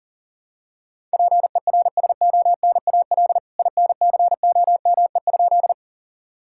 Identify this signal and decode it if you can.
Morse code, heard as a keyed tone, about 40 wpm, 700 Hz. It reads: PERSONAL INCOME?
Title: Sentences from Top 1000 Words - Review Part 5 - 40wpm